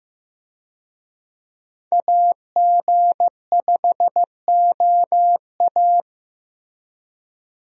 Morse code, heard as a keyed tone, about 16 wpm, 700 Hz. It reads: AG5OA
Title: USA Call Signs - 5 Characters - Repeated 2x - 15wpm